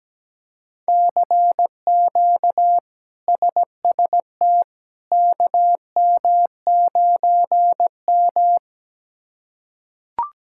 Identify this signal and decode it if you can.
Morse code, heard as a keyed tone, about 17 wpm, 700 Hz. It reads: CQ SST KM9M
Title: K1USN SST QSO Practice - 17wpm